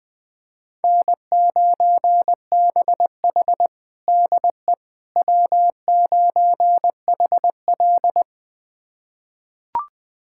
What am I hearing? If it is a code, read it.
Morse code, heard as a keyed tone, about 21 wpm, 700 Hz. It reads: N9BH DE W9HL
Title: USA Callsign DE Callsign - 4 Characters - 20wpm